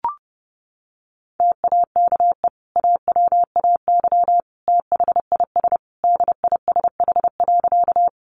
Morse code, heard as a keyed tone, about 30 wpm, 700 Hz. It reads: TAKE AWAY THIS DISH.
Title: Sentences from Top 1300 Words 30wpm